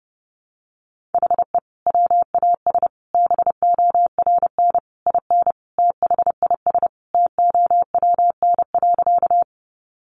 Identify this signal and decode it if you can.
Morse code, heard as a keyed tone, about 30 wpm, 700 Hz. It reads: HE WAS BORN IN THIS TOWN.
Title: Sentences from Top 1000 Words 30wpm